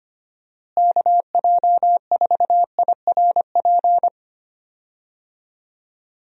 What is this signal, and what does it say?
Morse code, heard as a keyed tone, about 25 wpm, 700 Hz. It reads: KJ4IRP
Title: USA Call Signs - 6 Characters - Repeated 3x - 25wpm